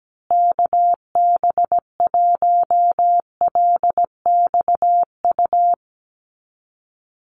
Morse code, heard as a keyed tone, about 17 wpm, 700 Hz. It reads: KB1LXU
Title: USA Call Signs Repeated 3x 17wpm